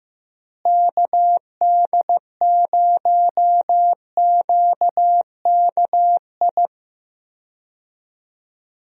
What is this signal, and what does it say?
Morse code, heard as a keyed tone, about 15 wpm, 700 Hz. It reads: KD0QKI
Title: USA Call Signs - 6 Characters - Repeated 2x - 15wpm